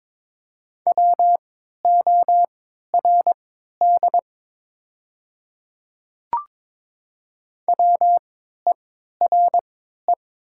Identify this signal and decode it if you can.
Morse code, heard as a keyed tone, about 22 wpm, spaced as for 12 wpm, 700 Hz. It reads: WORD WERE
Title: 3x Character Spacing - Top 100 Words - 22wpm